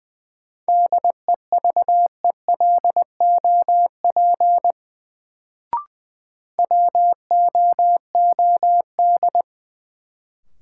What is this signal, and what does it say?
Morse code, heard as a keyed tone, about 20 wpm, 700 Hz. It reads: DEVELOP WOOD
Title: Sentences from Top 1000 Words - Review Part 2 - New Word Drill - 20wpm